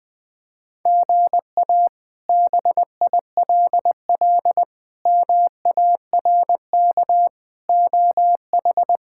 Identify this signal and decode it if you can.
Morse code, heard as a keyed tone, about 21 wpm, 700 Hz. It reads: GA BILL MARK OH
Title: K1USN SST QSO Practice - 20wpm